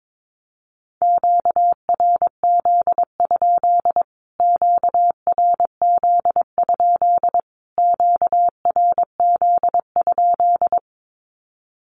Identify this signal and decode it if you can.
Morse code, heard as a keyed tone, about 22 wpm, 700 Hz. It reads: QRZ? QRZ? QRZ?
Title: Q-Codes Repeated 3x 22wpm